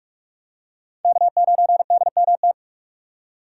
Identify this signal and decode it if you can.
Morse code, heard as a keyed tone, about 46 wpm, 700 Hz. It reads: K9DMT
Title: USA Call Signs - 5 Characters - 2x Character Spacing - 45wpm